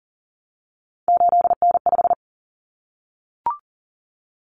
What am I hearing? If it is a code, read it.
Morse code, heard as a keyed tone, about 40 wpm, 700 Hz. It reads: 8N5
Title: Common Cut Numbers - 3-Digit 40wpm